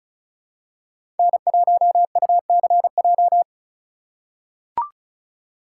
Morse code, heard as a keyed tone, about 36 wpm, 690 Hz. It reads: N1UCJ
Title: USA Call Signs - 5 Characters - 3x Character Spacing - 35wpm